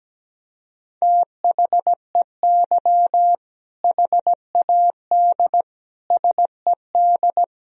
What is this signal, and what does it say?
Morse code, heard as a keyed tone, about 17 wpm, 700 Hz. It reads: THEY HAD SED
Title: Sentences from Top 400 Words and Abbreviations 17wpm